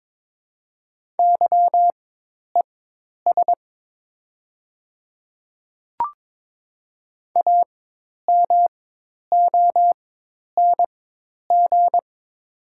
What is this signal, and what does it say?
Morse code, heard as a keyed tone, about 22 wpm, spaced as for 10 wpm, 700 Hz. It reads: YES AMONG
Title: 4x Character Spacing - Top 500 Words - 22wpm